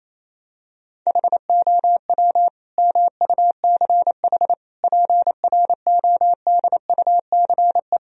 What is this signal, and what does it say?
Morse code, heard as a keyed tone, about 28 wpm, 700 Hz. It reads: HOW MUCH PRODUCE
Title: Sentences from Top 1000 Words - Review Part 3 - 28wpm